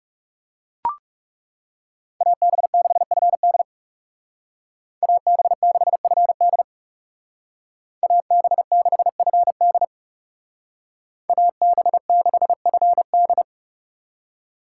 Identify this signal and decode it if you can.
Morse code, heard as a keyed tone, about 39 wpm, 700 Hz. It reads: AB6FD AB6FD AB6FD AB6FD